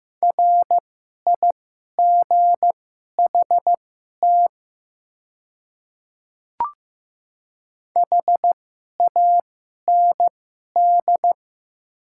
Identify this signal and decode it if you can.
Morse code, heard as a keyed tone, about 15 wpm, spaced as for 11 wpm, 700 Hz. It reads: RIGHT HAND
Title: 2x Character Spacing - Top 300 Words - 15wpm